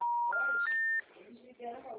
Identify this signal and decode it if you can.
Special information tone, with speech.
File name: out- -2029- - - .48554